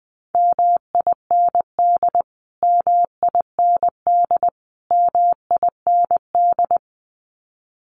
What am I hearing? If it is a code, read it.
Morse code, heard as a keyed tone, about 19 wpm, 700 Hz. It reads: MIND MIND MIND
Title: Top 500 Words Repeated 20wpm